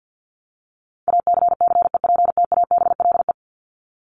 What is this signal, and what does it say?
Morse code, heard as a keyed tone, about 50 wpm, 700 Hz. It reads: ACCEPTABLE
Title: Top 1501-1750 Words 50wpm